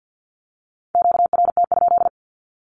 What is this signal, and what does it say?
Morse code, heard as a keyed tone, about 50 wpm, 700 Hz. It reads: QRT?
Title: Q-Codes with Spoken Meaning - 50wpm